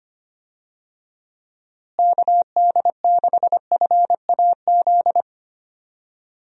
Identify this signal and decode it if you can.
Morse code, heard as a keyed tone, about 25 wpm, 700 Hz. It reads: KD6FAZ
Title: USA Call Signs - 6 Characters - Repeated 2x - 25wpm